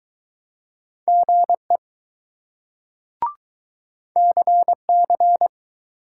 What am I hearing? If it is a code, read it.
Morse code, heard as a keyed tone, about 20 wpm, 700 Hz. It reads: GE CC